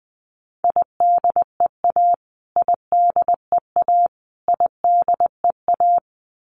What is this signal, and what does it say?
Morse code, heard as a keyed tone, about 20 wpm, 700 Hz. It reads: IDEA IDEA IDEA
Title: Top 500 Words Repeated 20wpm